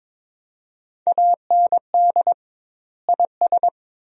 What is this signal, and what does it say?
Morse code, heard as a keyed tone, about 22 wpm, 700 Hz. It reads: AND IS
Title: 2x Word-Spacing - Sentences from Top 100 Words - 22wpm